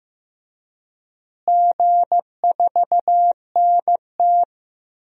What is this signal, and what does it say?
Morse code, heard as a keyed tone, about 15 wpm, 700 Hz. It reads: G4NT
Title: DXCC-100 Call Signs - 4 Characters - Repeated 2x - 15wpm